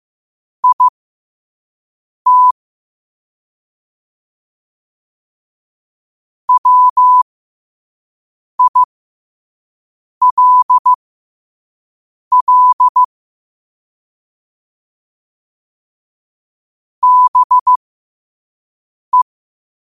Morse code, heard as a keyed tone, about 15 wpm, spaced as for 5 wpm, 1000 Hz. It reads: IT WILL BE
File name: qotdfiles / 5wpm